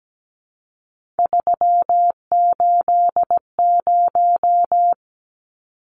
Morse code, heard as a keyed tone, about 17 wpm, 700 Hz. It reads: 380
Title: Numbers - 3-Digit 17wpm